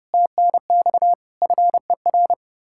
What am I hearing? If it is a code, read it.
Morse code, heard as a keyed tone, about 36 wpm, 700 Hz. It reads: TNX FER
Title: Speed-Racing Common QSO Phrases - 30wpm